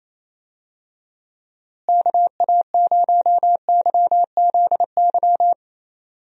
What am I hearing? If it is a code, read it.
Morse code, heard as a keyed tone, about 28 wpm, 700 Hz. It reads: KA0YZY
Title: USA Call Signs - 6 Characters - Repeated 3x - 28wpm